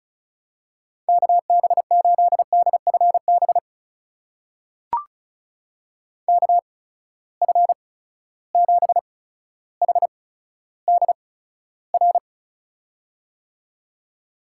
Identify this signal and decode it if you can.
Morse code, heard as a keyed tone, about 36 wpm, spaced as for 15 wpm, 700 Hz. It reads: KB8DFB KF7HDR
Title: USA Call Signs - 6 Characters - 8x Character Spacing - 35wpm